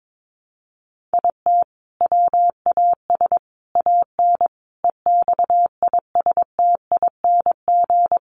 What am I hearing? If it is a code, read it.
Morse code, heard as a keyed tone, about 22 wpm, 700 Hz. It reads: IT WAS AN EXISTING
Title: Sentences from Top 1500 Words 22wpm